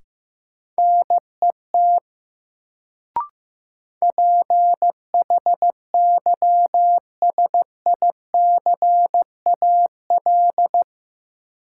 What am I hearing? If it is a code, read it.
Morse code, heard as a keyed tone, about 15 wpm, 700 Hz. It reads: NET PHYSICAL
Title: Sentences from Top 1400 Words - New Word Drill Alt - 15wpm